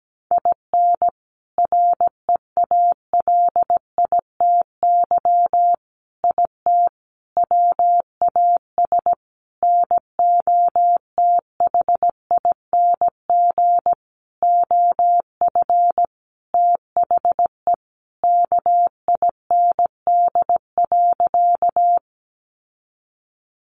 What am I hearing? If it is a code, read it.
Morse code, heard as a keyed tone, about 17 wpm, 700 Hz. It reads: IN REALITY IT WAS NOTHING OF THE KIND.
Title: Sentences from Top 1200 Words 17wpm